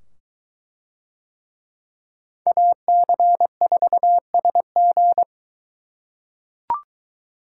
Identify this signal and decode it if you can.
Morse code, heard as a keyed tone, about 22 wpm, 700 Hz. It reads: AC4SG